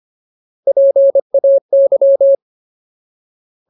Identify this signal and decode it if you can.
Morse code, heard as a keyed tone, about 25 wpm, 550 Hz. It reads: PAY
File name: Ultimate Word List / 25WPM Ultimate